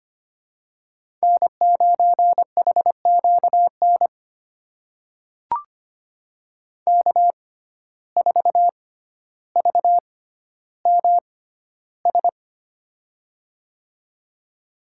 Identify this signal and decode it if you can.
Morse code, heard as a keyed tone, about 25 wpm, spaced as for 13 wpm, 700 Hz. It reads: N9HQN K4VMS
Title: USA Call Signs - 5 Characters - 6x Character Spacing - 25wpm